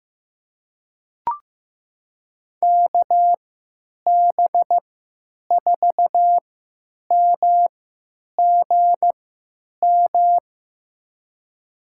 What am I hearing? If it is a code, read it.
Morse code, heard as a keyed tone, about 15 wpm, spaced as for 9 wpm, 700 Hz. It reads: KB4MGM
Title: USA Call Signs - 6 Characters - 3x Character Spacing - 15wpm